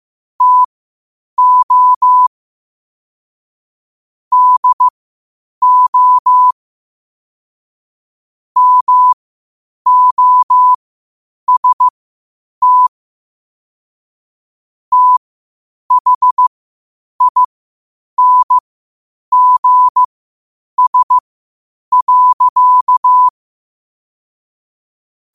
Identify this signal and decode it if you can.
Morse code, heard as a keyed tone, about 15 wpm, spaced as for 8 wpm, 1000 Hz. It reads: TO DO MOST THINGS.
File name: qotdfiles / 8wpm